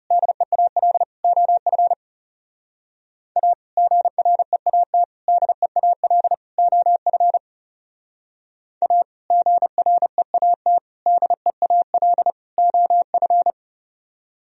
Morse code, heard as a keyed tone, about 39 wpm, 700 Hz. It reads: DEAL OF A GREAT DEAL OF A GREAT DEAL OF